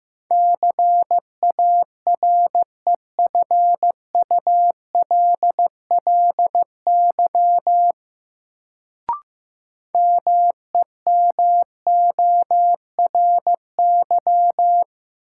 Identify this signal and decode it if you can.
Morse code, heard as a keyed tone, about 15 wpm, 700 Hz. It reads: CAREFULLY MEMORY
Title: Sentences from Top 1300 Words - New Word Drill - 15wpm